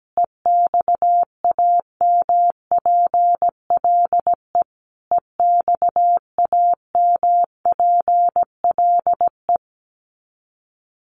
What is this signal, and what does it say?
Morse code, heard as a keyed tone, about 17 wpm, 700 Hz. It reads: EXAMPLE EXAMPLE
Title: Top 300 Words Repeated 17wpm